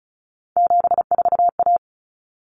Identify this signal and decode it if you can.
Morse code, heard as a keyed tone, about 34 wpm, 700 Hz. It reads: 74A
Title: Common Cut Numbers - 3-Digit 35wpm